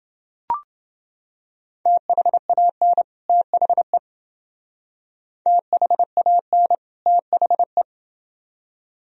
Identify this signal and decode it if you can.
Morse code, heard as a keyed tone, about 28 wpm, 700 Hz. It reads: THAN THE THAN THE